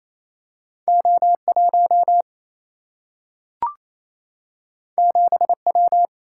Morse code, heard as a keyed tone, about 28 wpm, 700 Hz. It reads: O 1 7 W
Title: Single-Letter Single-Number 28wpm